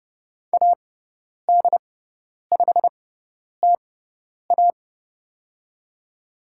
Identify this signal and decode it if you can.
Morse code, heard as a keyed tone, about 30 wpm, spaced as for 10 wpm, 700 Hz. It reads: AD5TA
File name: Lesson 17g - Letter u - Call Signs 30wpm